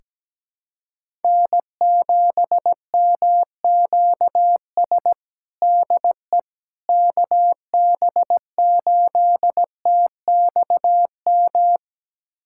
Morse code, heard as a keyed tone, about 17 wpm, 700 Hz. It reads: N7MQS DE KB8TXM
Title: USA Callsign DE Callsign 17wpm